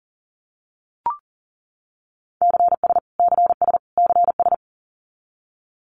Morse code, heard as a keyed tone, about 40 wpm, 700 Hz. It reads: CS CS CS
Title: Abbreviations Repeated 3x - 40wpm